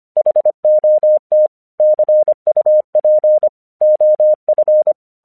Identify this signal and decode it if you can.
Morse code, heard as a keyed tone, about 25 wpm, 610 Hz. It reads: HOT CUP OF
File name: 1 English 25wpm / AQC08